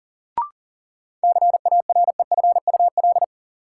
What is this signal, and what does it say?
Morse code, heard as a keyed tone, about 40 wpm, 700 Hz. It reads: CAREFUL
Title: Sentences from Top 1600 Words - New Word Drill Alt - 40wpm